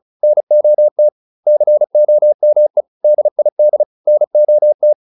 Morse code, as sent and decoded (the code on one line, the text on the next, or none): -. --- - / -.-. --- -- . / -.. .. -.. / -. --- -
NOT COME DID NOT